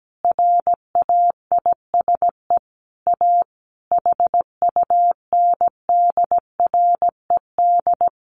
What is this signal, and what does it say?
Morse code, heard as a keyed tone, about 17 wpm, 700 Hz. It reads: RAISE A HUNDRED
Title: Sentences from Top 1400 Words 17wpm